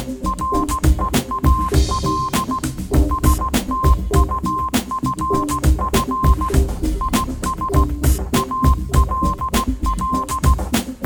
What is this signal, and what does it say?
Morse code, heard as a keyed tone, about 16 wpm, 1050 Hz. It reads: REAL STUFF IS ALL
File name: Kirkoid / Kirkoid - Small Stuff (Morse Mix)